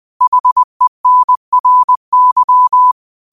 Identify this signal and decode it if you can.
Morse code, heard as a keyed tone, about 20 wpm, 1000 Hz. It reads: HENRY
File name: qotdfiles / 20wpm